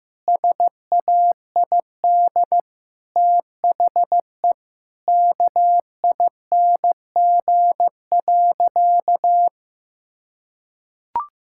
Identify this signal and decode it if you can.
Morse code, heard as a keyed tone, about 15 wpm, 700 Hz. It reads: SAID THE KING.
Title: Sentences from Top 1000 Words - Review Part 5 - 15wpm